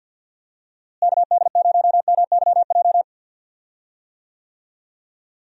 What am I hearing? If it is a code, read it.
Morse code, heard as a keyed tone, about 45 wpm, 700 Hz. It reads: KD0MYJ
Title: USA Call Signs - 6 Characters - Repeated 2x - 50wpm